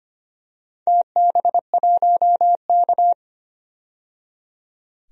Morse code, heard as a keyed tone, about 25 wpm, 700 Hz. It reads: TB1K
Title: DXCC-100 Call Signs - 4 Characters - 25wpm